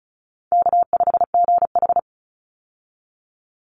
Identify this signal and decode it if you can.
Morse code, heard as a keyed tone, about 34 wpm, 700 Hz. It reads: K5GH
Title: Easy Call Signs Repeated 3x - 35wpm